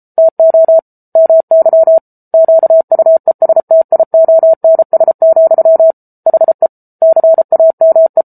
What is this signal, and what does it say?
Morse code, heard as a keyed tone, about 34 wpm, 650 Hz. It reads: TO MY QUESTIONS, HE CAME
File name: Book Two / 07-Chapter 7 - 33 WPM